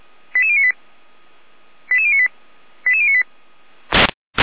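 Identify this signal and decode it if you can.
Five-tone selective call (ZVEI), heard as quick runs of five tones, tones of about 70 ms. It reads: 88098 88098 88098
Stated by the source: ZVEI five-tone selective call, address 88080, received on an SDR.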